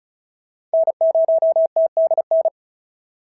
Morse code, heard as a keyed tone, about 36 wpm, 690 Hz. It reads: N0TDN
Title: SOTA QSO Practice 35wpm